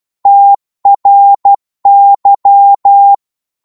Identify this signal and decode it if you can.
Morse code, heard as a keyed tone, about 12 wpm, 800 Hz. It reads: TRY
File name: 12wpm